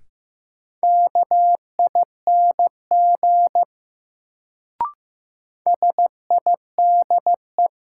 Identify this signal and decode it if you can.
Morse code, heard as a keyed tone, about 15 wpm, 700 Hz. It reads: KING SIDE
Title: Sentences from Top 400 Words - New Word Drill - 15wpm